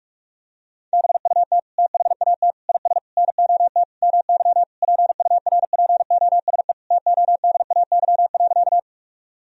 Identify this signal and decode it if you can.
Morse code, heard as a keyed tone, about 46 wpm, 700 Hz. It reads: BUT THAT IS NOT MY PURPOSE TODAY.
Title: Sentences from Top 1000 Words - Review Part 4 - 45wpm